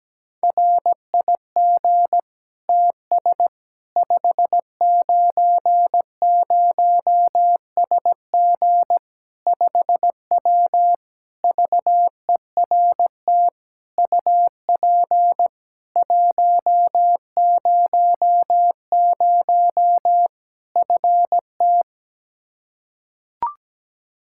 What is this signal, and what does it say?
Morse code, heard as a keyed tone, about 17 wpm, 700 Hz. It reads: RIG TS 590SG 5W VERT UP 100 FT
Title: Practice QSO - Exchange 2 - Info - 17wpm